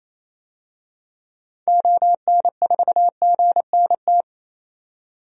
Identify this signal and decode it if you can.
Morse code, heard as a keyed tone, about 28 wpm, 700 Hz. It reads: ON4GNT
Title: DXCC-100 Call Signs - 6 Characters - 28wpm